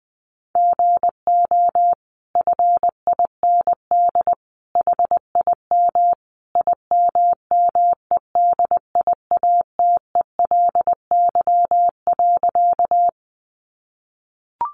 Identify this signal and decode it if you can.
Morse code, heard as a keyed tone, about 20 wpm, 700 Hz. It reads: GO FIND HIM IMMEDIATELY.
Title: Sentences from Top 1100 Words 20wpm